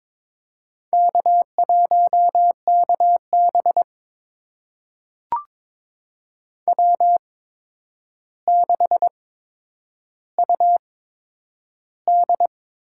Morse code, heard as a keyed tone, about 21 wpm, spaced as for 9 wpm, 700 Hz. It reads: K1KB W6UD
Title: USA Call Signs - 4 Characters - 8x Character Spacing - 22wpm